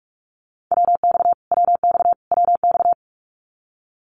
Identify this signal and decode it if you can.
Morse code, heard as a keyed tone, about 44 wpm, 700 Hz. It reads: WX WX WX
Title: Abbreviations Repeated 3x - 45wpm